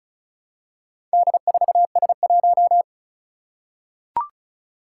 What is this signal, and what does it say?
Morse code, heard as a keyed tone, about 36 wpm, 700 Hz. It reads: D4S1
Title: Four Challenging Letter-Number Combinations 35wpm